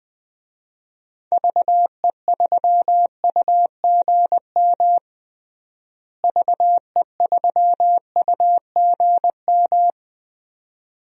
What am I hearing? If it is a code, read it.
Morse code, heard as a keyed tone, about 20 wpm, 700 Hz. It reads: VE3UGM VE3UGM
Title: DXCC-100 Call Signs - 6 Characters - Repeated 2x - 20wpm